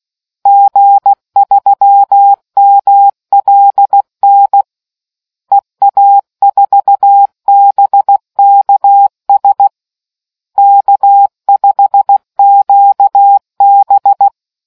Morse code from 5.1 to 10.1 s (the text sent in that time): EA4BKS